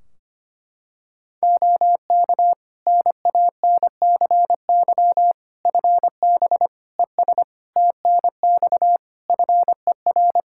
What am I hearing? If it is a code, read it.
Morse code, heard as a keyed tone, about 25 wpm, 700 Hz. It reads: OK NANCY FB ES TNX FER
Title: Practice QSO - Exchange 2 - Info - 25wpm